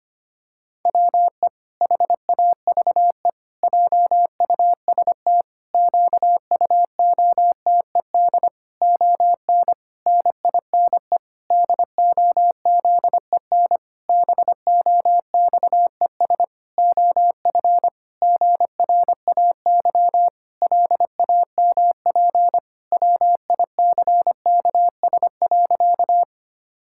Morse code, heard as a keyed tone, about 25 wpm, 700 Hz. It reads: WE HAVE JUST QUOTED ON NINE DOZEN BOXES OF GRAY LAMP WICKS.
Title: Pangram Challenge v2 - 25wpm